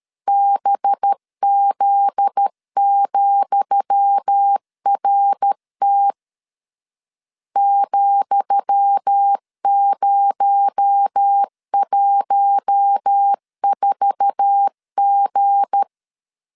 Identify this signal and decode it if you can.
Morse code, heard as a keyed tone, about 13 wpm, 800 Hz. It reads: BZ,RT ,014G
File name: LEZIONI / 10 SIGNS / 10D COMA 12wpm 2544